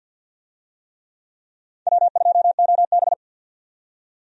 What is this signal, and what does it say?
Morse code, heard as a keyed tone, about 48 wpm, 700 Hz. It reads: W2OB
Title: USA Call Signs - 4 Characters - 50wpm